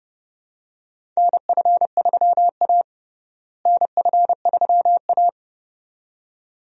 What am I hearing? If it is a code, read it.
Morse code, heard as a keyed tone, about 32 wpm, 700 Hz. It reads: NF3A NF3A
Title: USA Call Signs - 4 Characters - Repeated 2x - 30wpm